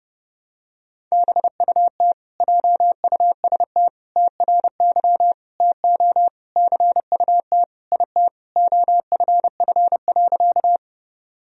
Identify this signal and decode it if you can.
Morse code, heard as a keyed tone, about 30 wpm, 700 Hz. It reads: BUT JUST TRY TO CUT IT OFF.
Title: Sentences from Top 1000 Words - Review Part 1 - 30wpm